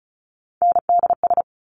Morse code, heard as a keyed tone, about 36 wpm, 700 Hz. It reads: NDS
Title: Top Three Letter Combinations in Words 35wpm